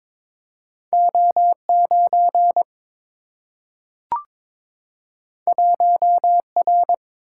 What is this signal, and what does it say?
Morse code, heard as a keyed tone, about 22 wpm, 700 Hz. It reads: O 9 1 R
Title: Single-Letter Single-Number 22wpm